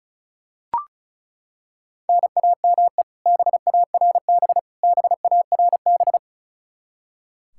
Morse code, heard as a keyed tone, about 36 wpm, 700 Hz. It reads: NAME BARB BARB
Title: Practice QSO - Exchange 1 - Beginning - 35wpm